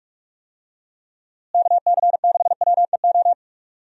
Morse code, heard as a keyed tone, about 41 wpm, 700 Hz. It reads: KC6WEO
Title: USA Call Signs - 6 Characters - Repeated 2x - 45wpm